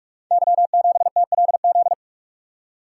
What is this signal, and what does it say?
Morse code, heard as a keyed tone, about 44 wpm, 690 Hz. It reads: Y7TLZ
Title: Code Groups - 5 Letter:Numbers - 45wpm